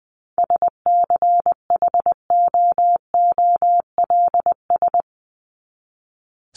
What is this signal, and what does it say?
Morse code, heard as a keyed tone, about 20 wpm, 700 Hz. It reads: SCHOOLS
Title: ES:S Suffix - Top 100 Words with Base Words 20wpm